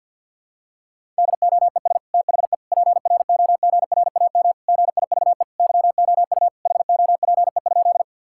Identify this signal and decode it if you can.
Morse code, heard as a keyed tone, about 52 wpm, 700 Hz. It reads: DOES THE PROGRAM GIVE YOU HOPE?
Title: Sentences from Top 1000 Words - Review Part 5 - 50wpm